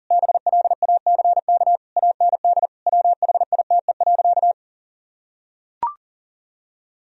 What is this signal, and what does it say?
Morse code, heard as a keyed tone, about 40 wpm, 700 Hz. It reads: BLACK AND WHITE.
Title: Sentences from Top 1000 Words - Review Part 2 - 40wpm